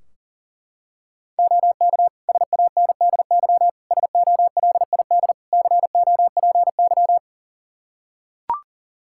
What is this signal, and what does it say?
Morse code, heard as a keyed tone, about 40 wpm, 700 Hz. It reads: OK SANDY SOLID COPY
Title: Practice QSO - Exchange 3 - More Info - 40wpm